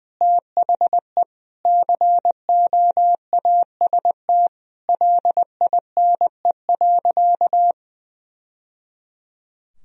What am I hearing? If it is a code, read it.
Morse code, heard as a keyed tone, about 20 wpm, 700 Hz. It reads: THE COAST LINE.
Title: Sentences from Top 1000 Words - Review Part 4 - 20wpm